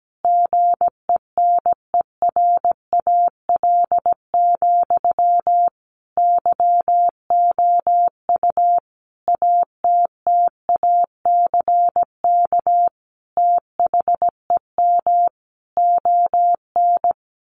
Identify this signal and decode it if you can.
Morse code, heard as a keyed tone, about 17 wpm, 700 Hz. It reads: GENERAL, YOU ATTACK THEM ON
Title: Sentences from Top 1500 Words 17wpm